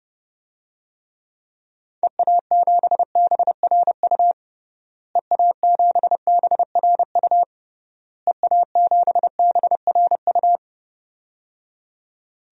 Morse code, heard as a keyed tone, about 31 wpm, 700 Hz. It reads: EA7BRU EA7BRU EA7BRU
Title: DXCC-100 Call Signs - 6 Characters - Repeated 3x - 30wpm